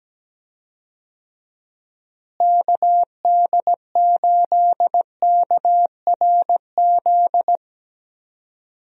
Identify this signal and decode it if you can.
Morse code, heard as a keyed tone, about 17 wpm, 700 Hz. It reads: KD8KRZ
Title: USA Call Signs - 6 Characters - Repeated 2x - 17wpm